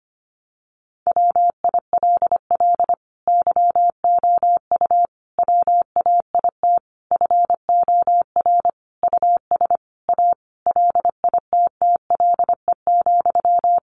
Morse code, heard as a keyed tone, about 25 wpm, 700 Hz. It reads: WILL YOU WAIT FOR US A LITTLE,
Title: Sentences from Top 900 Words 25wpm